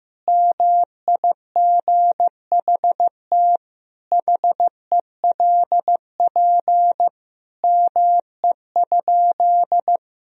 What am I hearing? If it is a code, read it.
Morse code, heard as a keyed tone, about 16 wpm, 700 Hz. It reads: MIGHT HELP ME?
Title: Sentences from Top 1000 Words - Review Part 3 - 15wpm